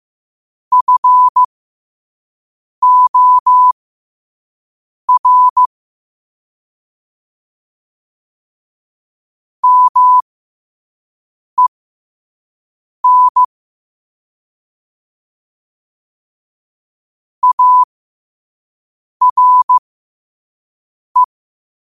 Morse code, heard as a keyed tone, about 15 wpm, spaced as for 5 wpm, 1000 Hz. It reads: FOR MEN ARE
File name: qotdfiles / 5wpm